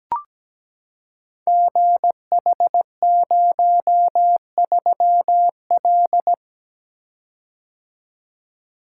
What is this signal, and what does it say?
Morse code, heard as a keyed tone, about 17 wpm, 690 Hz. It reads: GH03L
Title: Code Groups - 5 Letter:Numbers - 17wpm